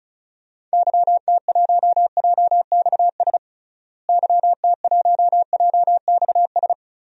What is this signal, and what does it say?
Morse code, heard as a keyed tone, about 36 wpm, 700 Hz. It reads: YT1JXS YT1JXS
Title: DXCC-100 Call Signs - 6 Characters - Repeated 2x - 35wpm